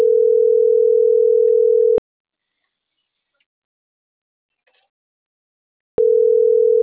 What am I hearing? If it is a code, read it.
Ringback tone (North America), 440 + 480 Hz.